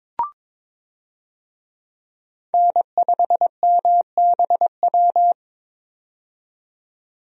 Morse code, heard as a keyed tone, about 22 wpm, 700 Hz. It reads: N5MBW
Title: USA Call Signs - 5 Characters - 22wpm